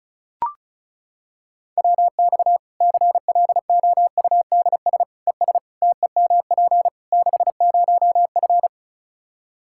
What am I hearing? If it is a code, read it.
Morse code, heard as a keyed tone, about 36 wpm, 700 Hz. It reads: WX CLOUDS ES TEMP 60F
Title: Practice QSO - Exchange 2 - Info - 35wpm